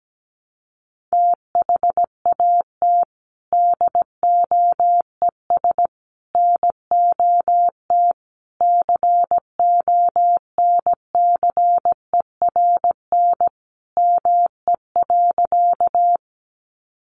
Morse code, heard as a keyed tone, about 17 wpm, 700 Hz. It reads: THAT DOES NOT CONCERN ME.
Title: Sentences from Top 1000 Words 17wpm